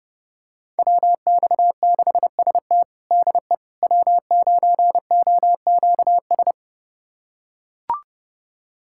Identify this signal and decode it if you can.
Morse code, heard as a keyed tone, about 32 wpm, 700 Hz. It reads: WX6ST DE W9OQS
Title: USA Callsign DE Callsign - 5 Characters - 30wpm